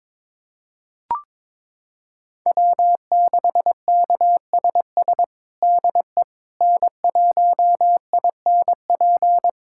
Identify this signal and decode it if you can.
Morse code, heard as a keyed tone, about 22 wpm, 700 Hz. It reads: W6KSS DE N1INP
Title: USA Callsign DE Callsign - 5 Characters - 22wpm